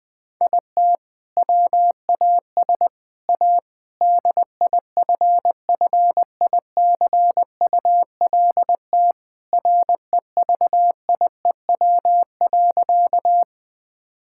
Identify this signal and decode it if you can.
Morse code, heard as a keyed tone, about 20 wpm, 700 Hz. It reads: IT WAS A DIFFICULT REVIEW.
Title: Sentences from Top 1000 Words - Review Part 6 - 20wpm